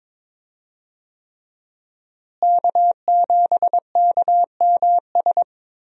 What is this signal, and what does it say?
Morse code, heard as a keyed tone, about 22 wpm, 700 Hz. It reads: K7KMS
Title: USA Call Signs - 5 Characters - Repeated 3x - 22wpm